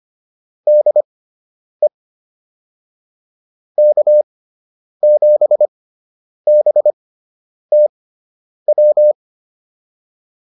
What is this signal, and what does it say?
Morse code, heard as a keyed tone, about 25 wpm, spaced as for 9 wpm, 600 Hz. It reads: DE K7BTW
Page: fundamental-practice-files